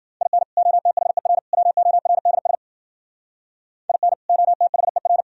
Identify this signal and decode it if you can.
Morse code, heard as a keyed tone, about 51 wpm, 700 Hz. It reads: IN OTHER WORDS IN OTHER